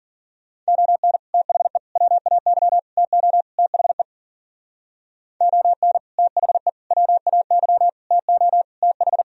Morse code, heard as a keyed tone, about 54 wpm, 700 Hz. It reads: ON THE WAY TO THE ON THE WAY TO TH